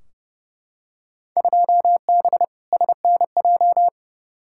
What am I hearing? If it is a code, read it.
Morse code, heard as a keyed tone, about 30 wpm, 700 Hz. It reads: 2B SNJ
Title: Field Day Exchanges Encore - 30wpm